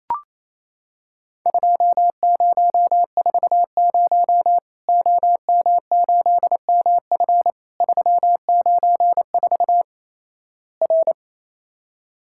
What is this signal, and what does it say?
Morse code, heard as a keyed tone, about 28 wpm, 700 Hz. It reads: 2040 OM8MF 394 R
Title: Worked All Europe DX Contest - QTC Exchange - 28wpm